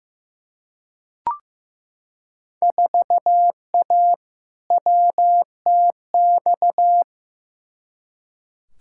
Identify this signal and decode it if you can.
Morse code, heard as a keyed tone, about 15 wpm, 700 Hz. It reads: 4A WTX
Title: Field Day Exchanges - 15wpm